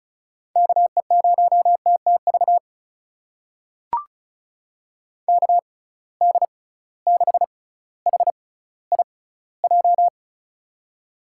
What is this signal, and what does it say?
Morse code, heard as a keyed tone, about 34 wpm, spaced as for 18 wpm, 700 Hz. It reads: KE0TTV KD6HIJ
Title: USA Call Signs - 6 Characters - 6x Character Spacing - 35wpm